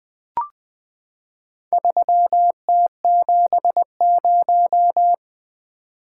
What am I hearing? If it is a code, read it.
Morse code, heard as a keyed tone, about 20 wpm, 700 Hz. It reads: 3T70
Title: Four Letter-Number Combinations - 20wpm